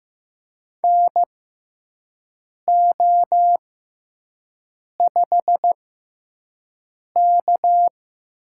Morse code, heard as a keyed tone, about 15 wpm, spaced as for 5 wpm, 700 Hz. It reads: NO5K
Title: USA Call Signs - 4 Characters - 6x Character Spacing - 15wpm